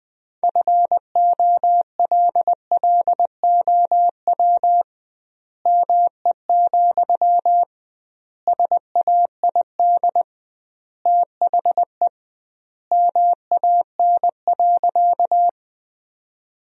Morse code, heard as a keyed tone, about 20 wpm, 700 Hz. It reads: FOLLOW ME, SAID THE MAN.
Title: 2x Word-Spacing - Sentences from Top 200 Words - 20wpm